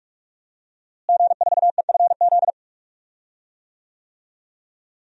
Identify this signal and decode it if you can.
Morse code, heard as a keyed tone, about 44 wpm, 700 Hz. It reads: G4EFZ
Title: DXCC-100 Call Signs - 5 Characters - Repeated 2x - 45wpm